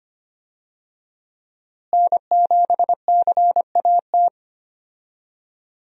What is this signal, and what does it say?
Morse code, heard as a keyed tone, about 25 wpm, 700 Hz. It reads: N7CAT
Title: USA Call Signs - 5 Characters - Repeated 3x - 25wpm